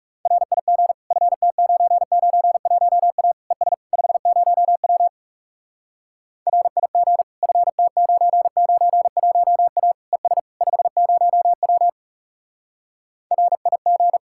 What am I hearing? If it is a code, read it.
Morse code, heard as a keyed tone, about 40 wpm, 700 Hz. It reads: RIG FT991A ES 50W RIG FT991A ES 50W RIG